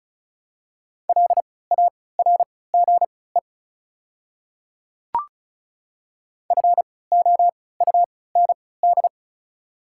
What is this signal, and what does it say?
Morse code, heard as a keyed tone, about 36 wpm, spaced as for 20 wpm, 690 Hz. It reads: LARGE FOUND
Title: 3x Character Spacing - Top 500 Words - 35wpm